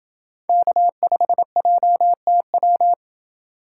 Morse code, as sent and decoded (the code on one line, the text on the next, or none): -.- ..... .--- - .--
K5JTW